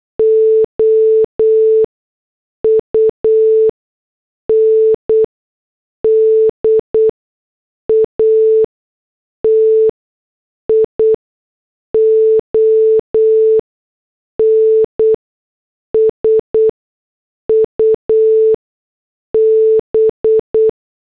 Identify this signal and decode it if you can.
Morse code, heard as a keyed tone, about 8 wpm, spaced as for 6 wpm, 440 Hz. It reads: OUNDATION SUB